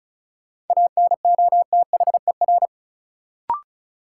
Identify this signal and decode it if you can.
Morse code, heard as a keyed tone, about 36 wpm, 700 Hz. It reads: ANOTHER
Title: Sentences from Top 600 Words - New Word Drill - 35wpm